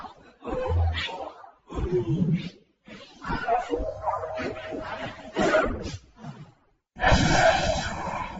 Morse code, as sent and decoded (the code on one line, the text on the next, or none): none